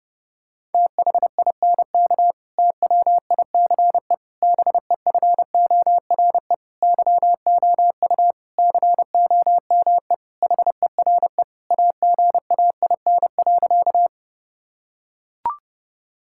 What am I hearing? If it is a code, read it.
Morse code, heard as a keyed tone, about 30 wpm, 700 Hz. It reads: THINK TWICE BEFORE YOU COME HERE AGAIN.
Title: Sentences from Top 1600 Words 30wpm